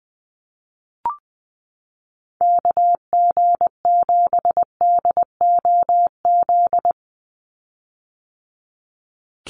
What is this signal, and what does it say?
Morse code, heard as a keyed tone, about 19 wpm, 700 Hz. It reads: KG7DOZ
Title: USA Call Signs 20wpm